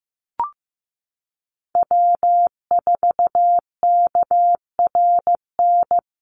Morse code, heard as a keyed tone, about 15 wpm, 700 Hz. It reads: W4KRN
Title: NA SOTA Chaser Callsigns 15wpm